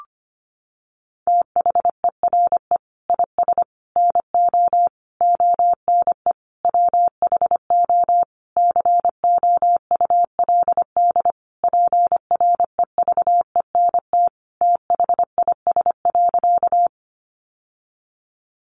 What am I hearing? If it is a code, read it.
Morse code, heard as a keyed tone, about 25 wpm, 700 Hz. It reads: THERE IS NO ONE WHO COULD PREVENT THIS.
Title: Sentences from Top 1100 Words 25wpm